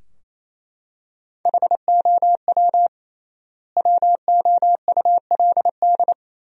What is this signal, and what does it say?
Morse code, heard as a keyed tone, about 28 wpm, 700 Hz. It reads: HOW WOULD
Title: 3x Word-Spacing - Sentences from Top 100 Words - 28wpm